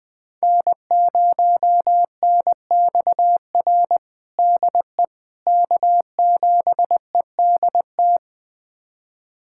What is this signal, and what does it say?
Morse code, heard as a keyed tone, about 20 wpm, 700 Hz. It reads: N0NXR DE K7EDT
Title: USA Callsign DE Callsign - 5 Characters - 20wpm